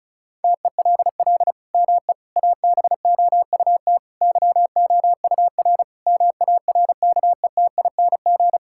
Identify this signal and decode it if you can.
Morse code, heard as a keyed tone, about 34 wpm, 700 Hz. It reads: TELL ME ABOUT YOUR MARKETING
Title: Sentences from Top 1000 Words - Review Part 5 - 35wpm